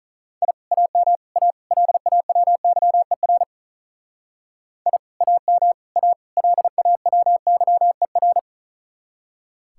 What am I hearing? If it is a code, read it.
Morse code, heard as a keyed tone, about 45 wpm, 700 Hz. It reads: I AM A LAWYER I AM A LAWYER